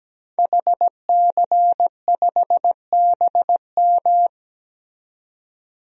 Morse code, heard as a keyed tone, about 17 wpm, 700 Hz. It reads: HC5BM
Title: DXCC-100 Call Signs - 5 Characters - 17wpm